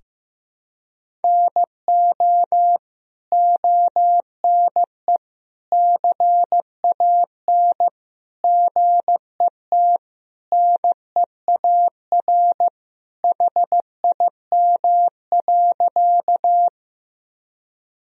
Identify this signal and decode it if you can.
Morse code, heard as a keyed tone, about 15 wpm, 700 Hz. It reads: NO ONE CAN GET NEAR HIM.
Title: Sentences from Top 1000 Words - Review Part 1 - 15wpm